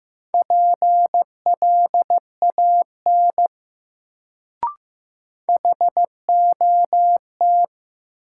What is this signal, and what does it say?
Morse code, heard as a keyed tone, about 15 wpm, 700 Hz. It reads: PLAN HOT
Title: Sentences from Top 500 Words - New Word Drill - 15wpm